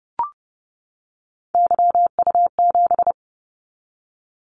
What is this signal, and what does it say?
Morse code, heard as a keyed tone, about 30 wpm, 700 Hz. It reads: YU7
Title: Callsign Prefixes for Top 25 DXCC - 30wpm